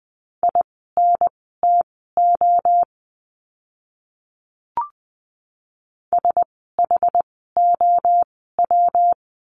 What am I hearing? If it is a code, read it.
Morse code, heard as a keyed tone, about 20 wpm, spaced as for 14 wpm, 700 Hz. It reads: INTO SHOW
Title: 206 Lesson 33f - Letter j - Words 20wpm